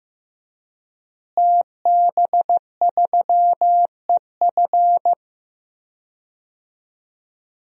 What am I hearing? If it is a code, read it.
Morse code, heard as a keyed tone, about 15 wpm, 700 Hz. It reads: TB3EF
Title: DXCC-100 Call Signs - 5 Characters - Repeated 3x - 15wpm